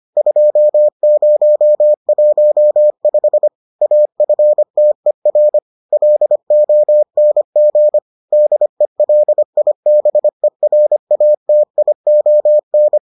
Morse code, heard as a keyed tone, about 25 wpm, 600 Hz. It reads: 2015 AFTER LONG DELIBERATION